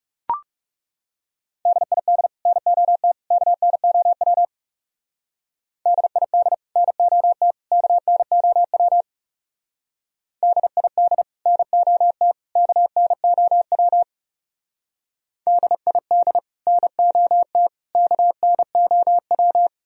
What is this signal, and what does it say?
Morse code, heard as a keyed tone, about 34 wpm, 700 Hz. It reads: DID NOT KNOW DID NOT KNOW DID NOT KNOW DID NOT KNOW
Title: Speed-Racing - Sets of 3 Words 30wpm